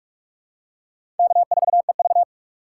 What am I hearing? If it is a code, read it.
Morse code, heard as a keyed tone, about 46 wpm, 700 Hz. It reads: K4EV
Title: USA Call Signs - 4 Characters - 2x Character Spacing - 45wpm